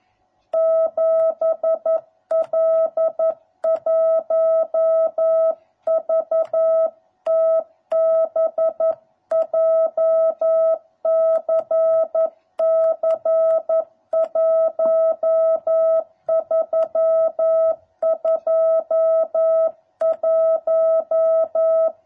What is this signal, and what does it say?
Morse code, heard as a keyed tone, about 11 wpm, 650 Hz. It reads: 7L1VTB JCC 1321